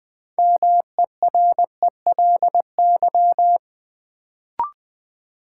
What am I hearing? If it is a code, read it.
Morse code, heard as a keyed tone, about 21 wpm, 700 Hz. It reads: MERELY
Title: Sentences from Top 1400 Words - New Word Drill - 20wpm